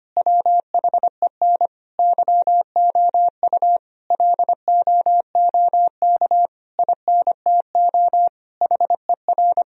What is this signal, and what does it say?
Morse code, heard as a keyed tone, about 25 wpm, 700 Hz. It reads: WHEN YOU LOOK INTO HER
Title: Sentences from Top 1000 Words - Review Part 4 - 25wpm